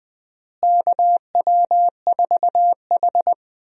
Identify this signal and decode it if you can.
Morse code, heard as a keyed tone, about 20 wpm, 700 Hz. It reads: KW4H
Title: USA Call Signs - 4 Characters - Repeated 3x - 20wpm